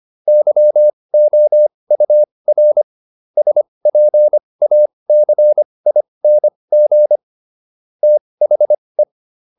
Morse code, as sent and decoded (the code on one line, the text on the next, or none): -.-- --- ..- .-. / ... .--. .- -.-. .. -. --. / - .... .
YOUR SPACING THE